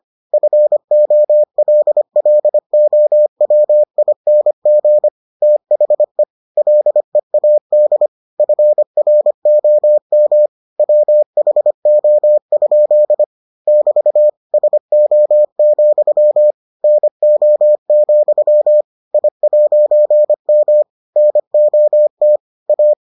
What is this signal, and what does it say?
Morse code, heard as a keyed tone, about 25 wpm, 600 Hz. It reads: FOLLOWING THE LEAD FROM WHO? <BT> SO, NO, I'M NOT A